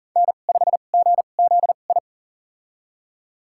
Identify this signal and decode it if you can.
Morse code, heard as a keyed tone, about 37 wpm, spaced as for 28 wpm, 700 Hz. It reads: N5GZI
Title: USA Call Signs - 5 Characters - 2x Character Spacing - 40wpm